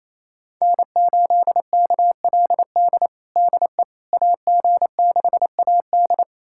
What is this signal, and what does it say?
Morse code, heard as a keyed tone, about 28 wpm, 700 Hz. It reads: N8KLD DE AG6AD
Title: USA Callsign DE Callsign - 5 Characters - 28wpm